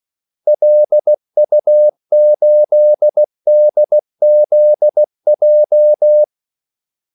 Morse code, heard as a keyed tone, about 16 wpm, 600 Hz. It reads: LU8DZJ
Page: Kaldesignaler 16 wpm